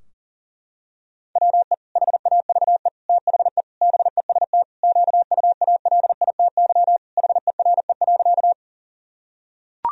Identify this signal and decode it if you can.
Morse code, heard as a keyed tone, about 40 wpm, 700 Hz. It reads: WE HAVE THE BEST QUALITY HERE.
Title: Sentences from Top 1000 Words - Review Part 4 - 40wpm